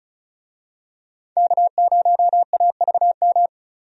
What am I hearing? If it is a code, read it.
Morse code, heard as a keyed tone, about 36 wpm, 700 Hz. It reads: K0AVM
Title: USA Call Signs - 5 Characters - 35wpm